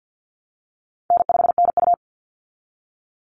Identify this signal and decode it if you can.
Morse code, heard as a keyed tone, about 49 wpm, 700 Hz. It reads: N5NU
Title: Call Signs Repeated 3x - 50wpm